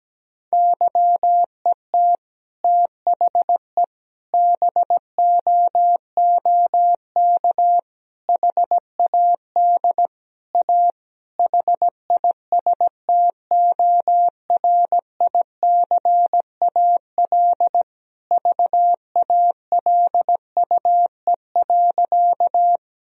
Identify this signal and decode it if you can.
Morse code, heard as a keyed tone, about 17 wpm, 700 Hz. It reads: YET THE BOOK HAD A HISTORICAL VALUE.
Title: Sentences from Top 1000 Words - Review Part 4 - 17wpm